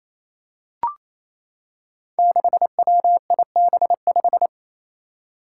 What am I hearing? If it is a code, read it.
Morse code, heard as a keyed tone, about 28 wpm, 700 Hz. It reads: 6WIB5
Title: Five Challenging Letter-Number Combinations 28wpm